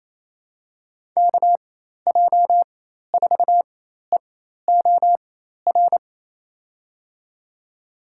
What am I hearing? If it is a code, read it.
Morse code, heard as a keyed tone, about 28 wpm, spaced as for 13 wpm, 700 Hz. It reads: KJ4EOR
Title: USA Call Signs - 6 Characters - 4x Character Spacing - 28wpm